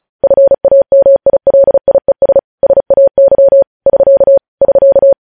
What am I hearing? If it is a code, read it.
Morse code, heard as a keyed tone, about 36 wpm, 550 Hz. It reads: FAMILIES SAY <SK> <SK>
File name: podcast / News headlines-1x-35x35